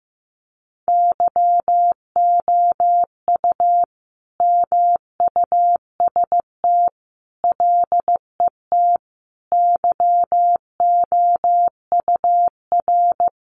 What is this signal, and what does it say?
Morse code, heard as a keyed tone, about 14 wpm, 700 Hz. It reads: YOU MUST LET YOUR
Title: Sentences from Top 1500 Words 15wpm